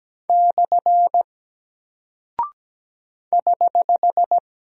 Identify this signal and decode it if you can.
Morse code, heard as a keyed tone, about 17 wpm, 700 Hz. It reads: / <HH>
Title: Single Character (Letter:Number:Punctuation:Prosign) Alt 17wpm